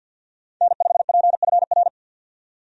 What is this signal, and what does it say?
Morse code, heard as a keyed tone, about 46 wpm, 700 Hz. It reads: N5PFR
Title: USA Call Signs - 5 Characters - 50wpm